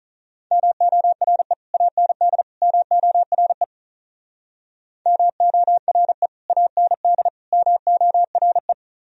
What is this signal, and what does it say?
Morse code, heard as a keyed tone, about 47 wpm, 700 Hz. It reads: MORE AND MORE MORE AND MORE